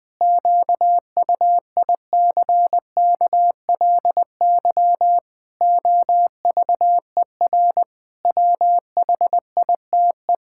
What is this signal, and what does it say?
Morse code, heard as a keyed tone, about 20 wpm, 700 Hz. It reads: QUICKLY OVER WHITE
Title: Pangram Challenge v2 - 20wpm